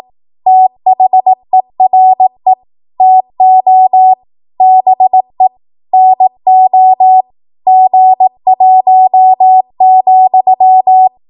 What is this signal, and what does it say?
Morse code, heard as a keyed tone, about 18 wpm, 750 Hz. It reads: THERE TO BE NO G1,
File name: Morse / 18WPM